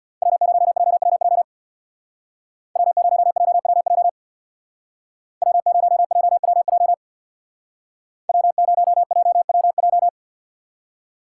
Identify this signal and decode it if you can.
Morse code, heard as a keyed tone, about 64 wpm, 700 Hz. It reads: W0JWJ W0JWJ W0JWJ W0JWJ